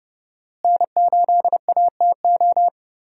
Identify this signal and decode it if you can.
Morse code, heard as a keyed tone, about 30 wpm, 700 Hz. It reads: N8ATO
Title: USA Call Signs - 5 Characters - Repeated 3x - 30wpm